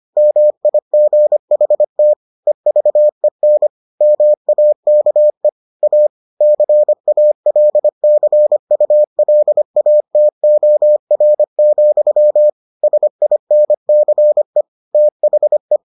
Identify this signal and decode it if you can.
Morse code, heard as a keyed tone, about 25 wpm, 600 Hz. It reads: MIGHT EVEN MAKE A CALCULATOR, SINCE THE